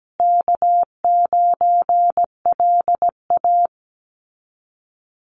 Morse code, heard as a keyed tone, about 17 wpm, 700 Hz. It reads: K9LA
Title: Easy Call Signs Repeated 3x - 17wpm